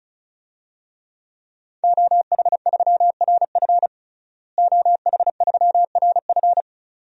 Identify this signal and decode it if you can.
Morse code, heard as a keyed tone, about 36 wpm, 700 Hz. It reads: OH3RF OH3RF
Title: DXCC-100 Call Signs - 5 Characters - Repeated 2x - 35wpm